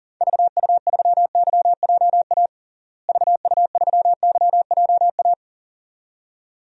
Morse code, heard as a keyed tone, about 37 wpm, 700 Hz. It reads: VU3YJA VU3YJA
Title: DXCC-100 Call Signs - 6 Characters - Repeated 2x - 40wpm